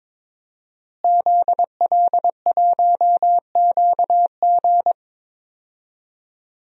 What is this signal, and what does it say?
Morse code, heard as a keyed tone, about 22 wpm, 700 Hz. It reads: ZL1QG
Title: DXCC-100 Call Signs - 5 Characters - Repeated 2x - 22wpm